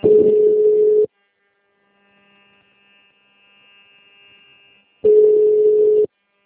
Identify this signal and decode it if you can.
Ringback tone (Europe, 425 Hz).
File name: out- -2004- - - .4